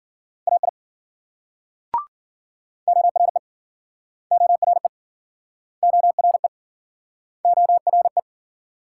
Morse code, heard as a keyed tone, about 53 wpm, 700 Hz. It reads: AI ORE ORE ORE ORE